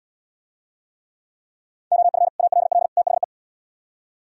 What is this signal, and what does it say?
Morse code, heard as a keyed tone, about 77 wpm, 700 Hz. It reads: OF ALL THE